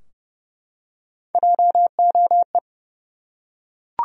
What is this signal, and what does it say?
Morse code, heard as a keyed tone, about 30 wpm, 700 Hz. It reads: JOE
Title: USA Common Names - Max 3 Letters - 30wpm